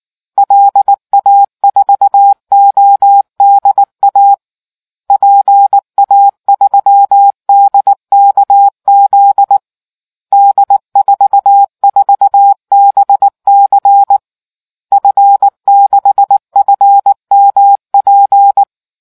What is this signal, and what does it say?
Morse code, heard as a keyed tone, about 19 wpm, 800 Hz. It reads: LA4ODA PA3DKZ D44BC F6FMP